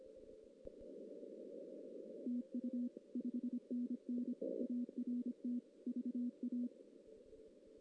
Morse code, heard as a keyed tone, about 25 wpm, 240 Hz. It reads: TU 5NN ART VA
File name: 2016 JAN CW NAQP / 2016 Jan naqp k5ot - 0-2